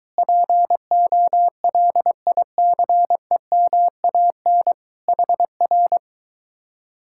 Morse code, heard as a keyed tone, about 23 wpm, 700 Hz. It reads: POLICEMAN HR